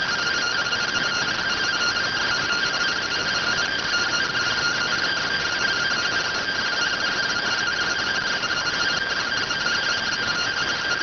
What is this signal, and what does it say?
Two-tone teleprinter signal, about 170 Hz shift.